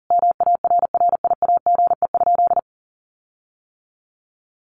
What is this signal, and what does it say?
Morse code, heard as a keyed tone, about 40 wpm, 700 Hz. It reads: MARRIAGE?
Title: Sentences from Top 1500 Words 40wpm